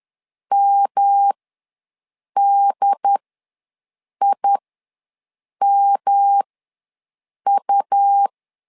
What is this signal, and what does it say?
Morse code, heard as a keyed tone, about 10 wpm, spaced as for 6 wpm, 800 Hz. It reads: MDIMU
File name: LEZIONI / 3 MWRO / 3 M ws 10wpm 2121